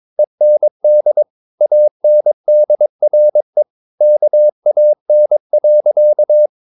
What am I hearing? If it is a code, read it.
Morse code, heard as a keyed tone, about 23 wpm, 600 Hz. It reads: END ANDRE KAN.
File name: Ferie QRS 22wpm